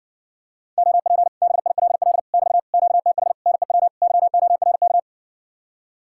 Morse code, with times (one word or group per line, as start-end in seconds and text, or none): 0.8-1.3 OP
1.4-2.2 BILL
2.3-2.6 <BT>
2.7-3.3 QTH
3.5-3.9 NEW
4.0-5.0 YORK